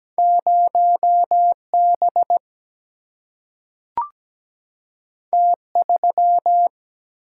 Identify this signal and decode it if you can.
Morse code, heard as a keyed tone, about 17 wpm, 700 Hz. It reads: 0B T3
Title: Single-Letter Single-Number 17wpm